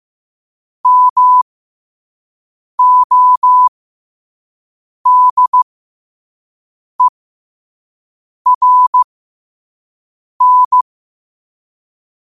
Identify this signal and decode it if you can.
Morse code, heard as a keyed tone, about 15 wpm, spaced as for 5 wpm, 1000 Hz. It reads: MODERN
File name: qotdfiles / 5wpm